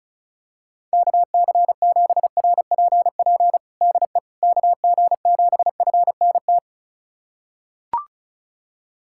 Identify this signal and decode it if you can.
Morse code, heard as a keyed tone, about 36 wpm, 700 Hz. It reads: KC7RPP DE KG7FNT
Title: USA Callsign DE Callsign - 6 Characters - 35wpm